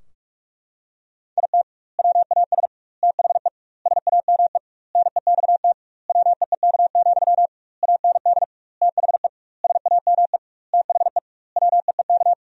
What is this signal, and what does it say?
Morse code, heard as a keyed tone, about 41 wpm, 700 Hz. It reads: IT WAS THE SAME NEXT WEEK, AND THE SAME THE WEEK
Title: 2x Word-Spacing - Sentences from Top 500 Words - 45wpm